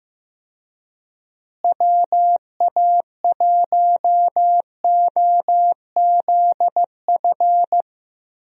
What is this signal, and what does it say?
Morse code, heard as a keyed tone, about 15 wpm, 700 Hz. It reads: WA1OZF
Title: USA Call Signs - 6 Characters - Repeated 3x - 15wpm